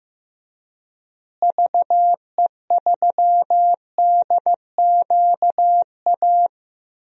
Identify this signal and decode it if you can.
Morse code, heard as a keyed tone, about 15 wpm, 700 Hz. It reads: VE3DQA
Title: DXCC-100 Call Signs - 6 Characters - Repeated 3x - 15wpm